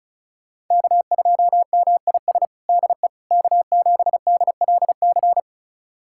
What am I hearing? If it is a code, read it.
Morse code, heard as a keyed tone, about 34 wpm, 700 Hz. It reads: K2MIS DE K7DLC
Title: USA Callsign DE Callsign - 5 Characters - 35wpm